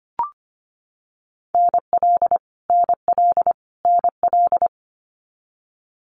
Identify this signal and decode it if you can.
Morse code, heard as a keyed tone, about 25 wpm, 700 Hz. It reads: NL NL NL
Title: Canadian Provinces and Territories - Repeated 3x - 25wpm